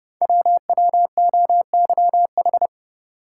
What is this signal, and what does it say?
Morse code, heard as a keyed tone, about 28 wpm, 700 Hz. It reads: WWOYH
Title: Code Groups - 5 Letters - 30wpm